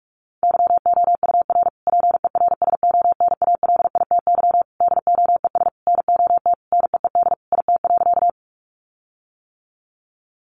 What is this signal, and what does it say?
Morse code, heard as a keyed tone, about 44 wpm, 700 Hz. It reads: YOUR PERSONALITY DOES NOT NEED IT.
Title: Sentences from Top 1400 Words 45wpm